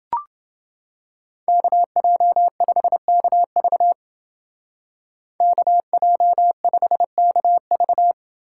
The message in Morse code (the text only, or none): KJ5KV KJ5KV